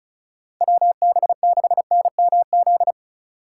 Morse code, heard as a keyed tone, about 36 wpm, 700 Hz. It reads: WB6NMZ
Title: USA Call Signs - 6 Characters - 2x Character Spacing - 35wpm